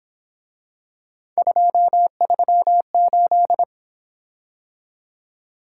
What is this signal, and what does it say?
Morse code, heard as a keyed tone, about 27 wpm, 700 Hz. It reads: 238
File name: Speed-Racing - Three Letter-Number Combinations - 22wpm